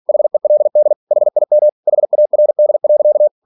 Morse code, heard as a keyed tone, about 46 wpm, 600 Hz. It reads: HELD HIM HARD.
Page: Never 47wpm